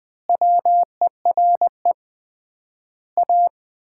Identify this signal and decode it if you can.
Morse code, heard as a keyed tone, about 21 wpm, 700 Hz. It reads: WERE A
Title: 3x Word-Spacing - Sets of 5 Words - 20wpm